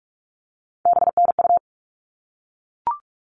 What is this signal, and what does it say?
Morse code, heard as a keyed tone, about 44 wpm, 700 Hz. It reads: BNU
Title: All Cut Numbers - 3-Digit 45wpm